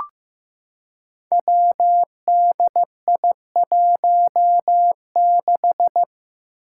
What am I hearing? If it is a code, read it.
Morse code, heard as a keyed tone, about 15 wpm, 700 Hz. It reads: WDI16
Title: Five Challenging Letter-Number Combinations 15wpm